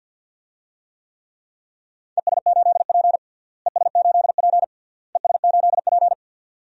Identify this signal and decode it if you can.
Morse code, heard as a keyed tone, about 48 wpm, 700 Hz. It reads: ES8P ES8P ES8P
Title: DXCC-100 Call Signs - 4 Characters - Repeated 3x - 50wpm